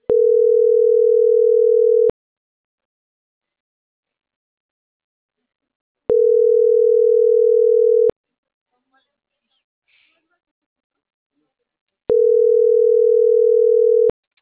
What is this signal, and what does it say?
Ringback tone (North America), 440 + 480 Hz.